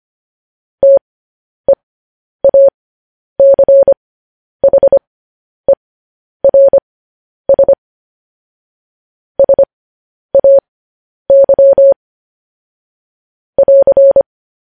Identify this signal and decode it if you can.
Morse code, heard as a keyed tone, about 25 wpm, spaced as for 10 wpm, 550 Hz. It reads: TEACHERS SAY <AR>
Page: news headlines in morse code at 10 wpm